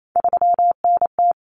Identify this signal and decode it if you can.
Morse code, heard as a keyed tone, about 28 wpm, 700 Hz. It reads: 3NT
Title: Common Cut Numbers - 3-Digit 28wpm